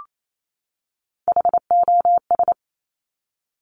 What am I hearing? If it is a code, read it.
Morse code, heard as a keyed tone, about 28 wpm, 700 Hz. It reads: HOS
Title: Top Three Letter Combinations in Words 28wpm